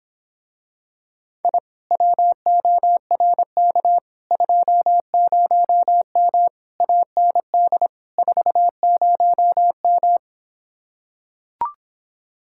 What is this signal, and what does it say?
Morse code, heard as a keyed tone, about 26 wpm, 700 Hz. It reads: I WORK 20M AND 40M